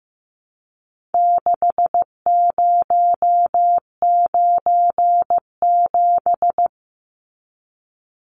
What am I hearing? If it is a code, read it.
Morse code, heard as a keyed tone, about 15 wpm, 700 Hz. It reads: 6097
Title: Numbers - 4-Digit 15wpm